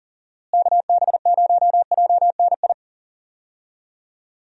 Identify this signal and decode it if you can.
Morse code, heard as a keyed tone, about 40 wpm, 700 Hz. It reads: KB0JNI
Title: USA Call Signs - 6 Characters - Repeated 2x - 40wpm